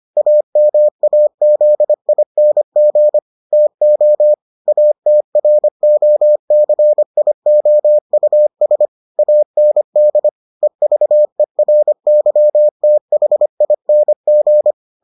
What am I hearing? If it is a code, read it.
Morse code, heard as a keyed tone, about 25 wpm, 600 Hz. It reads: AMAZING TO ATROCIOUS AND EVERYTHING